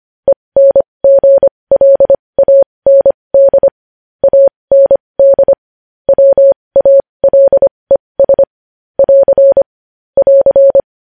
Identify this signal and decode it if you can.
Morse code, heard as a keyed tone, about 25 wpm, spaced as for 20 wpm, 550 Hz. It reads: ENGLAND AND WALES <AR> <AR>
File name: podcast / News headlines-1x-25x20